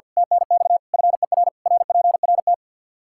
Morse code, heard as a keyed tone, about 53 wpm, 700 Hz. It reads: TNX FER RPRT
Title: Practice QSO - Exchange 2 - Info - 50wpm